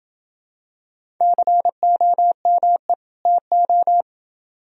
Morse code, heard as a keyed tone, about 27 wpm, 700 Hz. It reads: COME TO